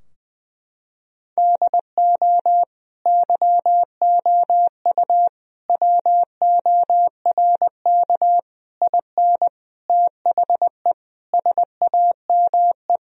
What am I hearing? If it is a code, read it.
Morse code, heard as a keyed tone, about 20 wpm, 700 Hz. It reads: DO YOU WORK IN THE SAME
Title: Sentences from Top 1000 Words - Review Part 4 - 20wpm